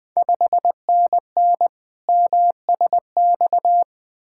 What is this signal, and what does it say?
Morse code, heard as a keyed tone, about 21 wpm, 700 Hz. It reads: 5NN MSX
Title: Ontario QSO Party Exchange 20wpm